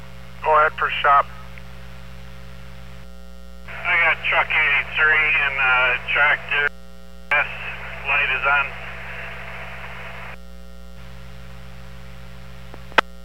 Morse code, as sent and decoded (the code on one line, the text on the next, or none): none